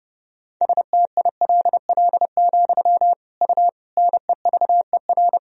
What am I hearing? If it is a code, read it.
Morse code, heard as a keyed tone, about 32 wpm, 700 Hz. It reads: STILL, U NEVER
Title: Sentences from Top 300 Words and Abbreviations 30wpm